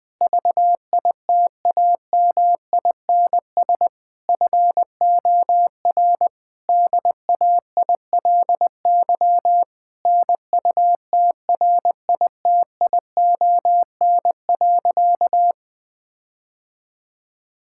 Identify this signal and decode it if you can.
Morse code, heard as a keyed tone, about 20 wpm, 700 Hz. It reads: VITAMINS FOR DAILY NUTRITION.
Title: Difficult Sentences with Hints - 20wpm